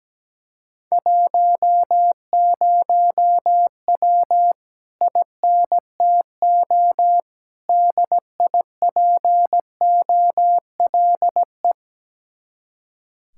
Morse code, heard as a keyed tone, about 17 wpm, 700 Hz. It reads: 10W INTO DIPOLE
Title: Speed-Racing Common QSO Phrases Encore - 17wpm